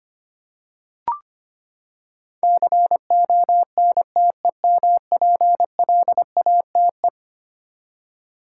Morse code, heard as a keyed tone, about 25 wpm, 700 Hz. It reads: CONTEMPLATE
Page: morse-code-ninja-practice-25wpm